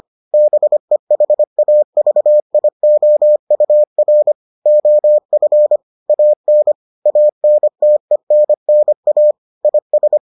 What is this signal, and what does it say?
Morse code, heard as a keyed tone, about 25 wpm, 600 Hz. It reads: BEHAVIOUR OF AN ANTENNA IS